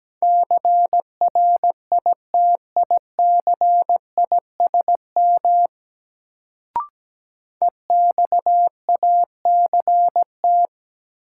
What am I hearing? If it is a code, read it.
Morse code, heard as a keyed tone, about 17 wpm, 700 Hz. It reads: CRITICISM EXACT
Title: Sentences from Top 1300 Words - New Word Drill Alt - 17wpm